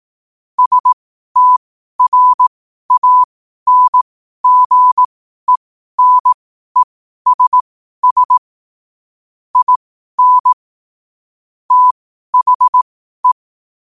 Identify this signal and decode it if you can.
Morse code, heard as a keyed tone, about 18 wpm, spaced as for 12 wpm, 1000 Hz. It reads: STRANGENESS IN THE
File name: qotdfiles / 12wpm